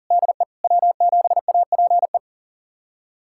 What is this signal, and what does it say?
Morse code, heard as a keyed tone, about 40 wpm, 700 Hz. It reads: DE W7APE
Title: USA Callsign DE Callsign 40wpm